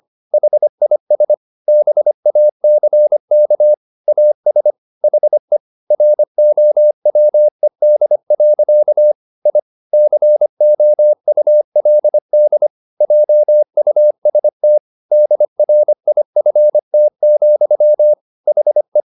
Morse code, heard as a keyed tone, about 25 wpm, 600 Hz. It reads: HIS BACK AS HE ROWED. I COULD JUST DRIFT, HE
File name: the old man and the sea part 1 25wpm